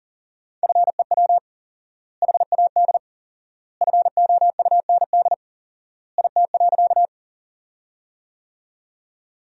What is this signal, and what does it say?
Morse code, heard as a keyed tone, about 40 wpm, 700 Hz. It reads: FEW HAD FOUND IT.
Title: 4x Word-Spacing - Sentences from Top 300 Words - 40wpm